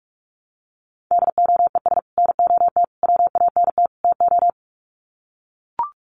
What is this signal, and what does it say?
Morse code, heard as a keyed tone, about 44 wpm, 700 Hz. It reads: DOES NOT WANT TO
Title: Sets of 4 Words 45wpm